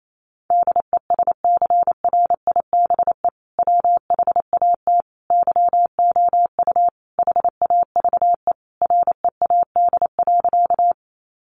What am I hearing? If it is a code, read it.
Morse code, heard as a keyed tone, about 28 wpm, 700 Hz. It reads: DESCRIBE WHAT YOU HAVE READ.
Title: Sentences from Top 1000 Words 28wpm